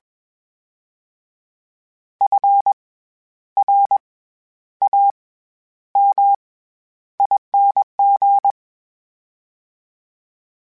Morse code, heard as a keyed tone, about 21 wpm, 800 Hz. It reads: F R A M ING
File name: Special INGs-mp3s / Special INGs-mp3s / SlowToFaster ing @20